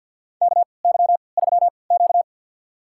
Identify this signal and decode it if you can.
Morse code, heard as a keyed tone, about 52 wpm, spaced as for 29 wpm, 700 Hz. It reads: KY3Q
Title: USA Call Signs - 4 Characters - 3x Character Spacing - 50wpm